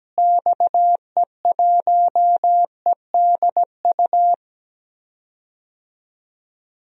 Morse code, heard as a keyed tone, about 17 wpm, 700 Hz. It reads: XE1EDU
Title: DXCC-100 Call Signs - 6 Characters - Repeated 3x - 17wpm